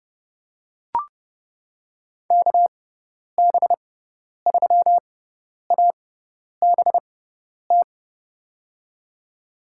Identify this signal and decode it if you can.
Morse code, heard as a keyed tone, about 30 wpm, spaced as for 10 wpm, 700 Hz. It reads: KB3ABT
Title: USA Call Signs - 6 Characters - 6x Character Spacing - 30wpm